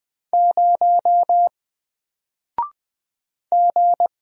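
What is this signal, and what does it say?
Morse code, heard as a keyed tone, about 20 wpm, 700 Hz. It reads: 0 G
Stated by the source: Single letter and number Morse drill at 20 wpm, from N4LCD.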